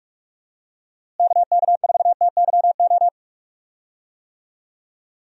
Morse code, heard as a keyed tone, about 46 wpm, 700 Hz. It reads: KK4TYO
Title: USA Call Signs - 6 Characters - Repeated 3x - 45wpm